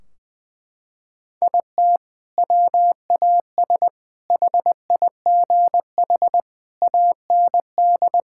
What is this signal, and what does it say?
Morse code, heard as a keyed tone, about 20 wpm, 700 Hz. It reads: IT WAS HIGH AND
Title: Sentences from Top 1600 Words 20wpm